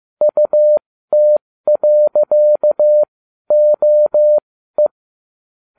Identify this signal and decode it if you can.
Morse code, heard as a keyed tone, about 15 wpm, spaced as for 12 wpm, 600 Hz. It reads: UT.OE